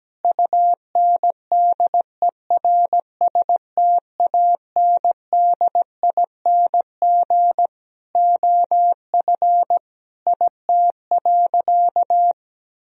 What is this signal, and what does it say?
Morse code, heard as a keyed tone, about 17 wpm, 700 Hz. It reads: UNDERSTANDING OF IT.
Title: Sentences from Top 1000 Words - Review Part 3 - 17wpm